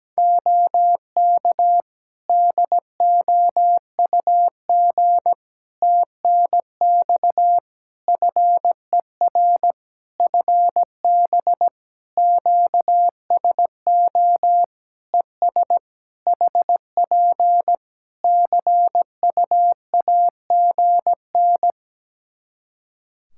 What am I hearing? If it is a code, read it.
Morse code, heard as a keyed tone, about 17 wpm, 700 Hz. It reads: OK DOUG TNX FER FB QSO ES HP CUAGN
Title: K1USN SST QSO Practice 17wpm